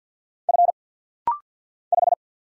Morse code, heard as a keyed tone, about 49 wpm, 700 Hz. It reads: F 5
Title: Single Letter-Number - Rapid-Fire 50wpm